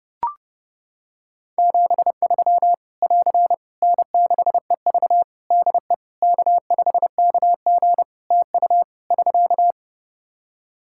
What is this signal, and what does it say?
Morse code, heard as a keyed tone, about 30 wpm, 700 Hz. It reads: 73 <AR> N6EV DE K5KG TU <SK>
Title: K1USN SST QSO Practice 30wpm